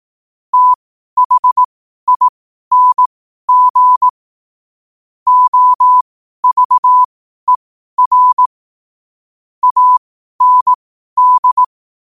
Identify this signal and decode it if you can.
Morse code, heard as a keyed tone, about 18 wpm, spaced as for 12 wpm, 1000 Hz. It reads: THING OVER AND